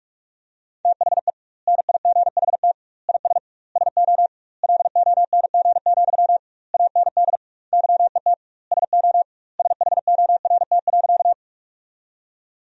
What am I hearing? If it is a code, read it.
Morse code, heard as a keyed tone, about 44 wpm, 700 Hz. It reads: THE NIGHT IS SO LONG, AND YET SO SHORT.
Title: 2x Word-Spacing - Sentences from Top 500 Words - 45wpm